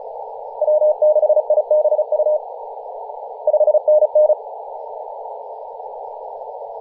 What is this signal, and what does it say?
Morse code, heard as a keyed tone, about 34 wpm, 610 Hz, with radio noise. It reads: W6IBU 5NN
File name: posts / 2017-nov-04-2233z-vk9ma-21-025mhz-cw